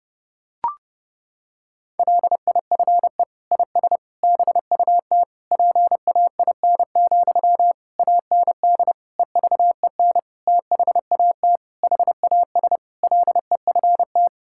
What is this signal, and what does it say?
Morse code, heard as a keyed tone, about 30 wpm, 700 Hz. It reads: LIFE IS BUT PAIN, AND EVEN THAT HAS LEFT
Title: Sentences from Top 1600 Words 30wpm